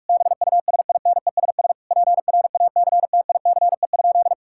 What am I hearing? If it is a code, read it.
Morse code, heard as a keyed tone, about 47 wpm, 700 Hz. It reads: BUSINESS PRACTICE?
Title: Sentences from Top 1000 Words - Review Part 6 - 45wpm